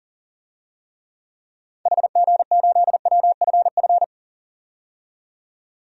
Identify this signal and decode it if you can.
Morse code, heard as a keyed tone, about 40 wpm, 700 Hz. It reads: HG8WFF
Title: DXCC-100 Call Signs - 6 Characters - 40wpm